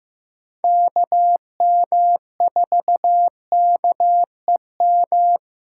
Morse code, heard as a keyed tone, about 15 wpm, 700 Hz. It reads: KM4KEM
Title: USA Call Signs - 6x Character Spacing - 15wpm